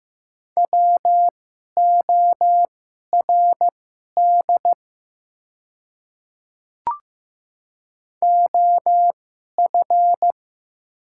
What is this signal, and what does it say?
Morse code, heard as a keyed tone, about 15 wpm, spaced as for 11 wpm, 700 Hz. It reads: WORD OF
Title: 2x Character Spacing - Top 100 Words - 15wpm